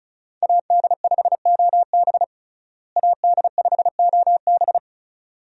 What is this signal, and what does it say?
Morse code, heard as a keyed tone, about 36 wpm, 700 Hz. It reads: AD5OB AD5OB
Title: USA Call Signs - 5 Characters - Repeated 2x - 35wpm